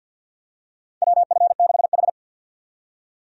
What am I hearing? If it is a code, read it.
Morse code, heard as a keyed tone, about 50 wpm, 700 Hz. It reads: WF6H
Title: USA Call Signs - 4 Characters - 50wpm